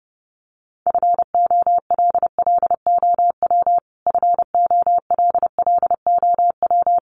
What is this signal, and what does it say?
Morse code, heard as a keyed tone, about 30 wpm, 700 Hz. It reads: FOLLOW FOLLOW
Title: Top 200 Words Repeated 30wpm